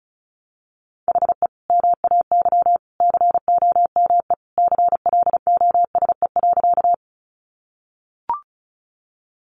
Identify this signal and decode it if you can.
Morse code, heard as a keyed tone, about 34 wpm, 700 Hz. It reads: HE MAY COME CLOSE.
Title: Sentences from Top 300 Words 35wpm